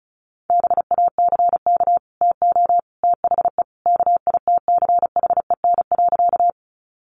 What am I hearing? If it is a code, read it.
Morse code, heard as a keyed tone, about 34 wpm, 700 Hz. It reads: BACK TO THE KITCHEN.
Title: Sentences from Top 1400 Words 35wpm